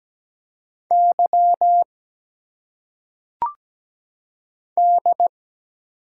Morse code, heard as a keyed tone, about 17 wpm, 700 Hz. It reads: Y D
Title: Single Letters - Phonetic Pronunciation 17wpm